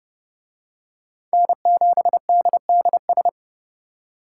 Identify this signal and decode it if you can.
Morse code, heard as a keyed tone, about 29 wpm, 700 Hz. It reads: N7DDS